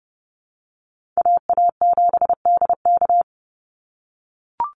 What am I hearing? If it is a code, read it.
Morse code, heard as a keyed tone, about 30 wpm, 700 Hz. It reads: AA7DK
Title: NA SOTA Chaser Callsigns 30wpm